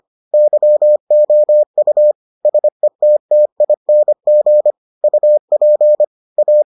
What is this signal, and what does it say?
Morse code, heard as a keyed tone, about 25 wpm, 600 Hz. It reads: YOU SETTING UP A